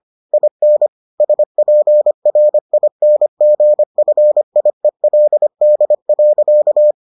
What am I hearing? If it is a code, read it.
Morse code, heard as a keyed tone, about 25 wpm, 600 Hz. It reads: IN SPRINGFIELD.